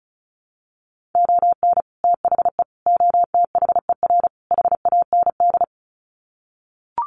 Morse code, heard as a keyed tone, about 34 wpm, 700 Hz. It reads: ON THE OTHER HAND
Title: Sets of 4 Words 35wpm